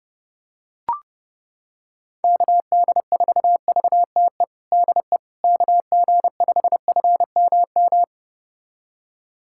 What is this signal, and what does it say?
Morse code, heard as a keyed tone, about 30 wpm, 700 Hz. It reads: KD4VTE DE KG5FMM
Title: USA Callsign DE Callsign - 6 Characters - 30wpm